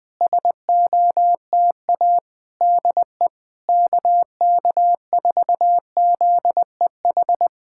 Morse code, heard as a keyed tone, about 20 wpm, 700 Hz. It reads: SOTA DE KK4ZEH
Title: SOTA QSO Practice 20wpm